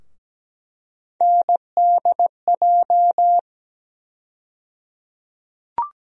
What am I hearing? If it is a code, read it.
Morse code, heard as a keyed tone, about 17 wpm, 700 Hz. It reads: NDJ
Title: USA Call Sign Postfixes - 3 Letters 17wpm